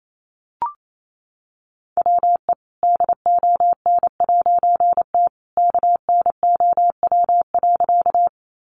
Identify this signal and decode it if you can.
Morse code, heard as a keyed tone, about 28 wpm, 700 Hz. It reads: WE DON'T KNOW.
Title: A Sound Of Thunder 28wpm